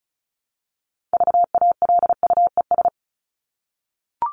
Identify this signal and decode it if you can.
Morse code, heard as a keyed tone, about 34 wpm, 700 Hz. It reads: VALUES
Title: ES:S Suffix - Top 100 Words with Base Words 35wpm